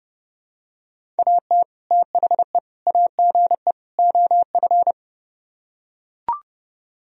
Morse code, heard as a keyed tone, about 30 wpm, 700 Hz. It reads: AT THE AGE OF